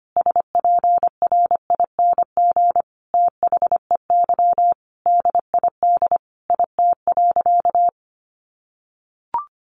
Morse code, heard as a keyed tone, about 25 wpm, 700 Hz. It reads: SPRING THEY DID IT.
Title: Sentences from Top 1200 Words 25wpm